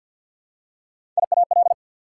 Morse code, heard as a keyed tone, about 54 wpm, 700 Hz. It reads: IAL
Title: Prefixes and Suffixes 50wpm